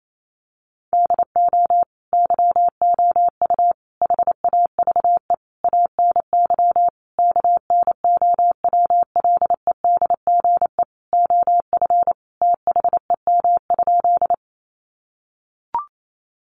Morse code, heard as a keyed tone, about 28 wpm, 700 Hz. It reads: DO YOU HAVE ANY KNOWLEDGE OF THEM?
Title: Sentences from Top 600 Words 28wpm